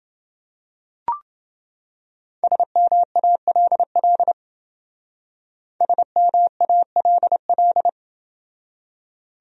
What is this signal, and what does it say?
Morse code, heard as a keyed tone, about 28 wpm, 700 Hz. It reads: SMALL SMALL